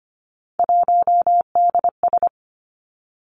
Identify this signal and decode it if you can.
Morse code, heard as a keyed tone, about 25 wpm, 700 Hz. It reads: 1DS
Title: Three Challenging Letter-Number Combinations 25wpm